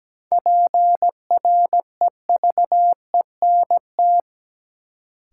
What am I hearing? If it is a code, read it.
Morse code, heard as a keyed tone, about 17 wpm, 700 Hz. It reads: PREVENT
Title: Sentences from Top 1100 Words - New Word Drill Alt - 17wpm